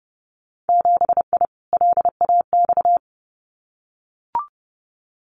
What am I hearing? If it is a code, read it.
Morse code, heard as a keyed tone, about 30 wpm, 700 Hz. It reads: 7I LAX
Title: Winter Field Day Exchanges Encore 30wpm